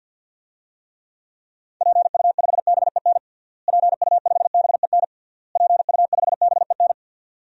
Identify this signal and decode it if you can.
Morse code, heard as a keyed tone, about 52 wpm, 700 Hz. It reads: PU5BEN PU5BEN PU5BEN
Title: DXCC-100 Call Signs - 6 Characters - Repeated 3x - 50wpm